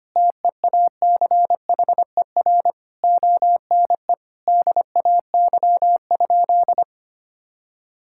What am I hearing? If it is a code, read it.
Morse code, heard as a keyed tone, about 25 wpm, 700 Hz. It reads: TEACHER ONE DAY?
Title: Sentences from Top 1000 Words - Review Part 6 - 25wpm